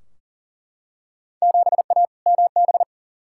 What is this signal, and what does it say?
Morse code, heard as a keyed tone, about 40 wpm, 700 Hz. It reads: 7A MB
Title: Field Day Exchanges Encore - 40wpm